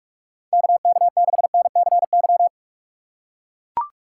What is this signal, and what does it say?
Morse code, heard as a keyed tone, about 44 wpm, 700 Hz. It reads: KK6NCY